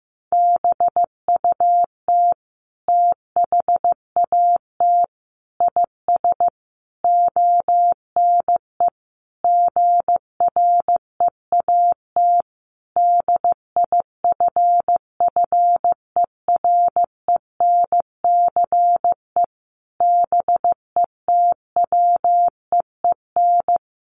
Morse code, heard as a keyed tone, about 15 wpm, 700 Hz. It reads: BUT THAT IS ONE GREAT DIFFERENCE BETWEEN
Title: Sentences from Top 800 Words 15wpm